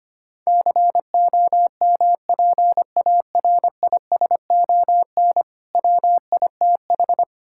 Morse code, heard as a keyed tone, about 25 wpm, 700 Hz. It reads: COMPARISON WITH
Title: Sentences from Top 1600 Words 25wpm